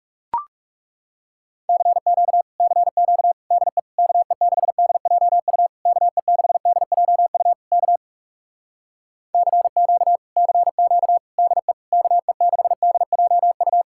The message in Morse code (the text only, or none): CQ CQ DE KE6DJU KE6DJU K CQ CQ DE KE6DJU